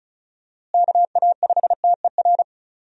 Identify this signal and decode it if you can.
Morse code, heard as a keyed tone, about 34 wpm, 700 Hz. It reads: KA5TER
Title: USA Call Signs - 6 Characters - Repeated 3x - 35wpm